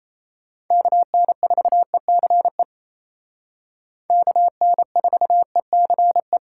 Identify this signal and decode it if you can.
Morse code, heard as a keyed tone, about 36 wpm, 700 Hz. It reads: KN4ECE KN4ECE